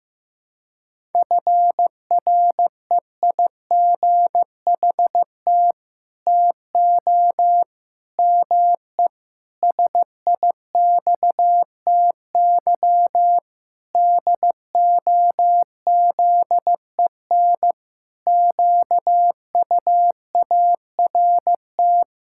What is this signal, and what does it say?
Morse code, heard as a keyed tone, about 15 wpm, 700 Hz. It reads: FREIGHT TO ME SIXTY DOZEN QUART
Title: Pangram Challenge v2 - 15wpm